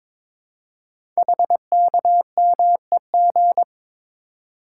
Morse code, heard as a keyed tone, about 22 wpm, 700 Hz. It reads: HKMEG
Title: Code Groups - 5 Letters - 22wpm